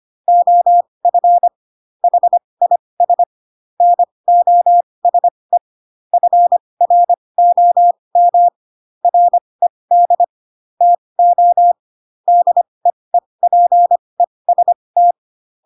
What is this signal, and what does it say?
Morse code, heard as a keyed tone, about 25 wpm, spaced as for 20 wpm, 700 Hz. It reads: OF HIS NOSE FROM RED TO DEEPEST
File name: Pinocchio / Pinocchio ch1 15 25 ik1hge